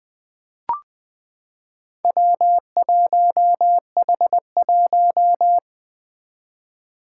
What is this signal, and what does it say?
Morse code, heard as a keyed tone, about 20 wpm, 700 Hz. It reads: W1H1
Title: Four Challenging Letter-Number Combinations 20wpm